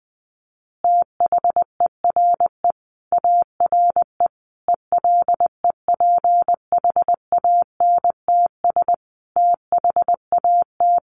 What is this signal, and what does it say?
Morse code, heard as a keyed tone, about 19 wpm, 700 Hz. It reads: THERE ARE ELEPHANTS THAT
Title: Easy Phrases 20wpm